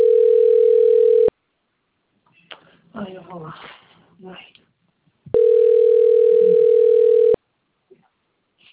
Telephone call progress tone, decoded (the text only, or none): ringback tone